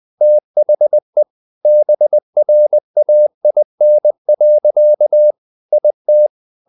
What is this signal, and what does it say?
Morse code, heard as a keyed tone, about 20 wpm, 600 Hz. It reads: THE BRAIN. IT
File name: the old man and the sea part 2 20wpm